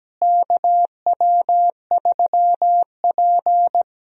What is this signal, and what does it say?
Morse code, heard as a keyed tone, about 17 wpm, 700 Hz. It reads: KW3P
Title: USA Call Signs - 4 Characters - Repeated 2x - 17wpm